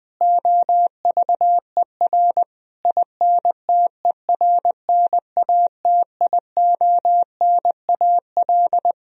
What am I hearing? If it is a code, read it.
Morse code, heard as a keyed tone, about 20 wpm, 700 Hz. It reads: OVER INTERNATIONAL
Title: Sentences from Top 1000 Words - Review Part 4 - 20wpm